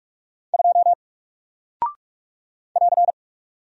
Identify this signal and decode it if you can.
Morse code, heard as a keyed tone, about 44 wpm, 700 Hz. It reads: 2 <AR>
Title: Single Character (Letter:Number:Punctuation:Prosign) Alt 45wpm